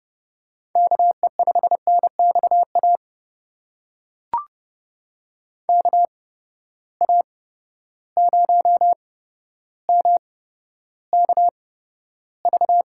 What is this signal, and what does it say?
Morse code, heard as a keyed tone, about 30 wpm, spaced as for 13 wpm, 700 Hz. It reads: KE5NXA KA0MKV
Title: USA Call Signs - 6 Characters - 8x Character Spacing - 30wpm